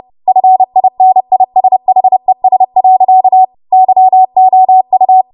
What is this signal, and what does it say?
Morse code, heard as a keyed tone, about 30 wpm, 750 Hz. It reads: FINISHES. YOU
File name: Morse / 30WPM